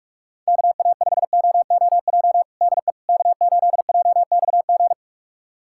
Morse code, heard as a keyed tone, about 41 wpm, 700 Hz. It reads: KA5OOJ DE K8JXG
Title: USA Callsign DE Callsign 45wpm